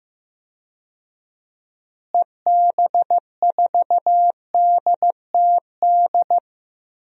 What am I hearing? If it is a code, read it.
Morse code, heard as a keyed tone, about 15 wpm, 700 Hz. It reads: EB4DTD
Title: DXCC-100 Call Signs - 6 Characters - Repeated 3x - 15wpm